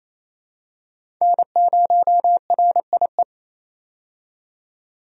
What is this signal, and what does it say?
Morse code, heard as a keyed tone, about 28 wpm, 700 Hz. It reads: N0RIE
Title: USA Call Signs - 5 Characters - 28wpm